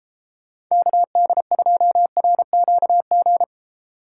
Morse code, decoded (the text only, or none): KD2RQG